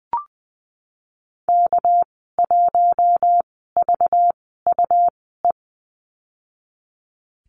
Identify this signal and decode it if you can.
Morse code, heard as a keyed tone, about 20 wpm, spaced as for 15 wpm, 700 Hz. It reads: K1VUE
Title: 207 Lesson 33g - Letter j - Call Signs 20wpm